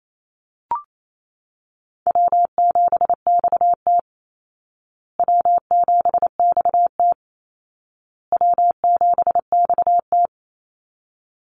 Morse code, heard as a keyed tone, about 28 wpm, 700 Hz. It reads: W7XT W7XT W7XT
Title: Easy Call Signs Repeated 3x - 28wpm